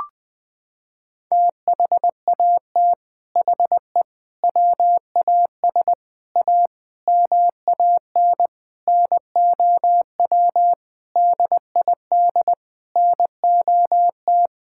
Morse code, heard as a keyed tone, about 21 wpm, 700 Hz. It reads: THAT HE WAS A MAN NOW DID NOT
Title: Sentences from Top 1000 Words - Review Part 6 - 20wpm